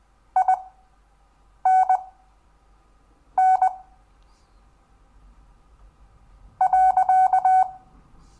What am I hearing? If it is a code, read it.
Morse code, heard as a keyed tone, about 20 wpm, spaced as for 4 wpm, 750 Hz. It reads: INN.
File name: cw / Lesson10 REVIEW